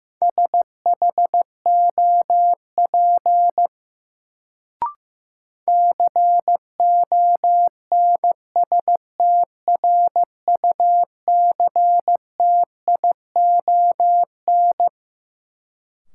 Morse code, heard as a keyed tone, about 15 wpm, 700 Hz. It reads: SHOP CONSTRUCTION
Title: Sentences from Top 1600 Words - New Word Drill Alt - 15wpm